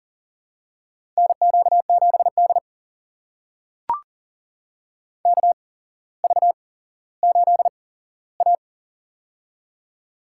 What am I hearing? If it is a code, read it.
Morse code, heard as a keyed tone, about 43 wpm, spaced as for 17 wpm, 700 Hz. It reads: NQ7D KV8A
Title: USA Call Signs - 4 Characters - 8x Character Spacing - 40wpm